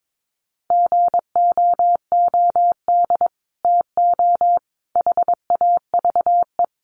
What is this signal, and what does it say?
Morse code, heard as a keyed tone, about 22 wpm, 700 Hz. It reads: GOOD TO HAVE
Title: Sets of 5 Words Encore2 22wpm